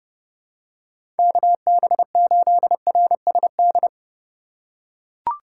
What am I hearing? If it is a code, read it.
Morse code, heard as a keyed tone, about 30 wpm, 700 Hz. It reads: KB8RSD
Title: USA Call Signs - 6 Characters - 4x Character Spacing - 30wpm